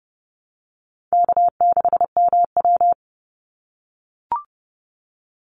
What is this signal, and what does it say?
Morse code, heard as a keyed tone, about 30 wpm, 700 Hz. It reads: K6MW
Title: NA SOTA Chaser Callsigns 30wpm